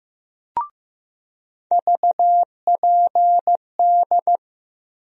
Morse code, heard as a keyed tone, about 15 wpm, 700 Hz. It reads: VPD
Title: USA Call Sign Postfixes - 3 Letters 15wpm